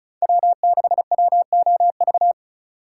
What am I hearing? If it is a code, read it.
Morse code, heard as a keyed tone, about 36 wpm, 700 Hz. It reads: W6WOV
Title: USA Call Signs - 5 Characters - 4x Character Spacing - 35wpm